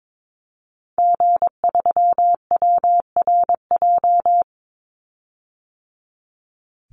Morse code, heard as a keyed tone, about 22 wpm, 700 Hz. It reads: G3WRJ
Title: Call Signs - 22wpm